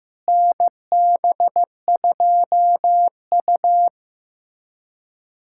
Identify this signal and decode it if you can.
Morse code, heard as a keyed tone, about 15 wpm, 700 Hz. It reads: NB2U
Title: USA Callsign DE Callsign - 4 Characters - 15wpm